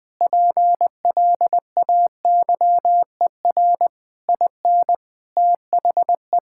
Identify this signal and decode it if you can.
Morse code, heard as a keyed tone, about 20 wpm, 700 Hz. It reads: PLAYER IN THE
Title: Sentences with 6-Word Phrases - 20wpm